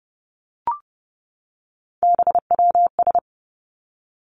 Morse code, heard as a keyed tone, about 30 wpm, 700 Hz. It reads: BWS
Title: Three Challenging Letter-Number Combinations 30wpm